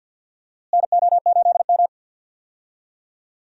Morse code, heard as a keyed tone, about 47 wpm, 700 Hz. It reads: NO8M
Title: USA Call Signs - 4 Characters - Repeated 2x - 50wpm